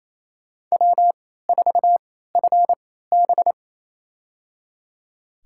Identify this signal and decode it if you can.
Morse code, heard as a keyed tone, about 28 wpm, spaced as for 16 wpm, 700 Hz. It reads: W4FB
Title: USA Call Signs - 4 Characters - 3x Character Spacing - 28wpm